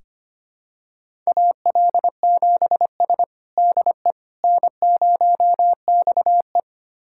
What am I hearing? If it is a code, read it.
Morse code, heard as a keyed tone, about 25 wpm, 700 Hz. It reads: AL7S DE N0XE
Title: USA Callsign DE Callsign - 4 Characters - 25wpm